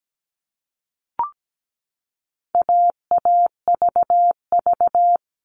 Morse code, heard as a keyed tone, about 17 wpm, 700 Hz. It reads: AAVV
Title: All Cut Numbers - 4-Digit 17wpm